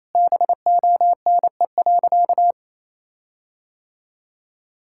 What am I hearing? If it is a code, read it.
Morse code, heard as a keyed tone, about 28 wpm, 700 Hz. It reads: BONE.
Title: Sentences from Top 1600 Words 28wpm